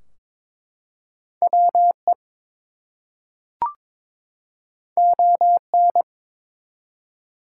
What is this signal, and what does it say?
Morse code, heard as a keyed tone, about 22 wpm, 700 Hz. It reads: WE ON
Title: Two Letter Words - 22wpm